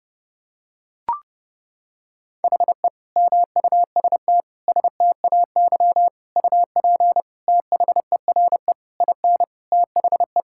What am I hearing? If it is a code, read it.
Morse code, heard as a keyed tone, about 32 wpm, 700 Hz. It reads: HE MUST STAY UP THERE IN THE
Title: Sentences from Top 1600 Words 30wpm